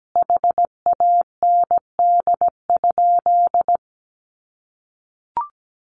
Morse code, heard as a keyed tone, about 17 wpm, 700 Hz. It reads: HAND?
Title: Sentences from Top 1500 Words 17wpm